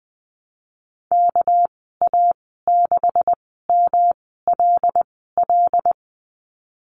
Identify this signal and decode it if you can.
Morse code, heard as a keyed tone, about 20 wpm, spaced as for 14 wpm, 700 Hz. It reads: KA6MLL
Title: 200 Lesson 32g - Letter k - Call Signs 20wpm